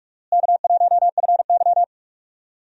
Morse code, heard as a keyed tone, about 46 wpm, 700 Hz. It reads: K1FY
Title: USA Call Signs - 4 Characters - Repeated 2x - 45wpm